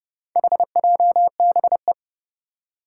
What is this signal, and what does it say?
Morse code, heard as a keyed tone, about 32 wpm, 700 Hz. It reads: HJBE
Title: Four Challenging Letter-Number Combinations 30wpm